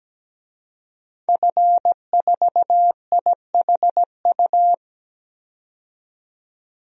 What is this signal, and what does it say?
Morse code, heard as a keyed tone, about 17 wpm, 700 Hz. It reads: F4IHU
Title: DXCC-100 Call Signs - 5 Characters - 17wpm